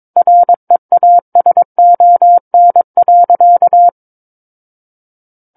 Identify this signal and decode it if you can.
Morse code, heard as a keyed tone, about 22 wpm, 700 Hz. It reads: REASON.
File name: Book One / 13-Chapter 13 - 22 WPM